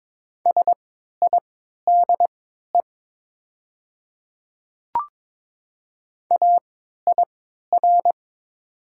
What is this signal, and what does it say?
Morse code, heard as a keyed tone, about 22 wpm, spaced as for 12 wpm, 700 Hz. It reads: SIDE AIR
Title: 3x Character Spacing - Top 300 Words - 22wpm